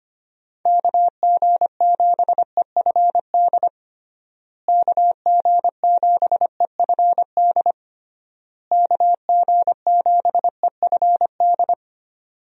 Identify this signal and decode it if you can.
Morse code, heard as a keyed tone, about 25 wpm, 700 Hz. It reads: KG7EFD KG7EFD KG7EFD
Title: USA Call Signs - 6 Characters - Repeated 3x - 25wpm